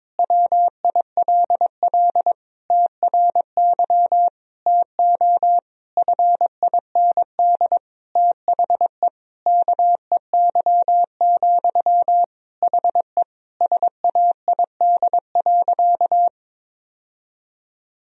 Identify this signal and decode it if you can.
Morse code, heard as a keyed tone, about 22 wpm, 700 Hz. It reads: WILL TRY TO FIND THE KEY, HE SAID.
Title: Sentences from Top 1000 Words - Review Part 4 - 22wpm